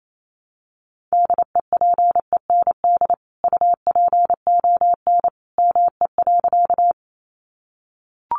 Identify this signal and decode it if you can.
Morse code, heard as a keyed tone, about 28 wpm, 700 Hz. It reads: DEPEND UPON ME.
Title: Sentences from Top 1300 Words 28wpm